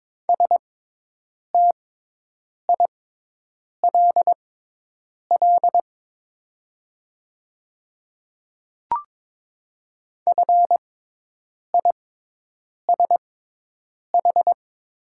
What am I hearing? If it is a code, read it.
Morse code, heard as a keyed tone, about 22 wpm, spaced as for 8 wpm, 700 Hz. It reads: STILL FISH
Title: 6x Character Spacing - Top 300 Words - 22wpm